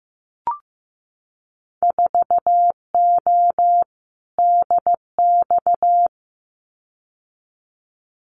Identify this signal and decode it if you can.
Morse code, heard as a keyed tone, about 15 wpm, 700 Hz. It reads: 4O DX
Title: Winter Field Day Exchanges Encore 15wpm